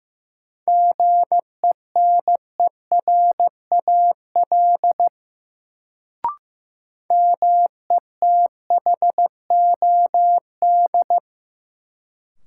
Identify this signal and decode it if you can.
Morse code, heard as a keyed tone, about 15 wpm, 700 Hz. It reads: GENERAL METHOD
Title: Sentences from Top 600 Words - New Word Drill - 15wpm